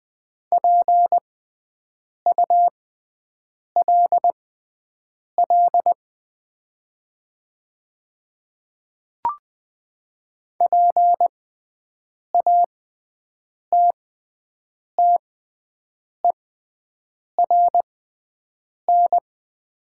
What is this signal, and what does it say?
Morse code, heard as a keyed tone, about 20 wpm, spaced as for 7 wpm, 700 Hz. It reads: PULL PATTERN
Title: 6x Character Spacing - Top 500 Words - 20wpm